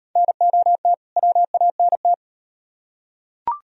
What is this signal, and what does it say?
Morse code, heard as a keyed tone, about 38 wpm, 700 Hz. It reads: NOT WANT